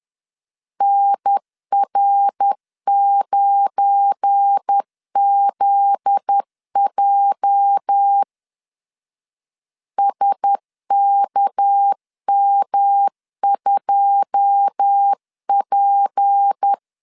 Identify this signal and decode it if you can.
Morse code, heard as a keyed tone, about 10 wpm, 800 Hz. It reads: NR9ZJ SKM2P
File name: LEZIONI / 9 2 10wpm 2736